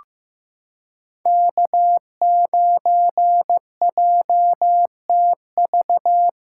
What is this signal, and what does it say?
Morse code, heard as a keyed tone, about 15 wpm, 700 Hz. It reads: K9JTV
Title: USA Callsign DE Callsign - 5 Characters - 15wpm